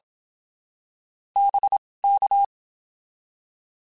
Morse code, heard as a keyed tone, about 27 wpm, spaced as for 19 wpm, 800 Hz. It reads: BK
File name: (spaced) / QSOs Spaced-mp3s / WD0BGP 1 @25 sp